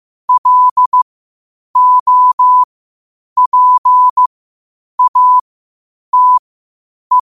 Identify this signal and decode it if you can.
Morse code, heard as a keyed tone, about 15 wpm, spaced as for 8 wpm, 1000 Hz. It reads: LOPATE
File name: qotdfiles / 8wpm